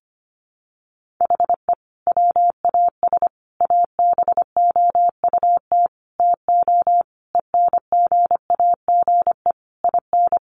Morse code, heard as a keyed tone, about 25 wpm, 700 Hz. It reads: HE WAS ABOUT TO ENGAGE IN
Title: Sentences from Top 1500 Words 25wpm